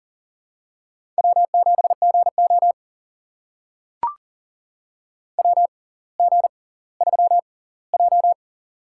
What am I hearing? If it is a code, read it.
Morse code, heard as a keyed tone, about 40 wpm, spaced as for 20 wpm, 700 Hz. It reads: W7GO WG3J
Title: USA Call Signs - 6x Character Spacing - 40wpm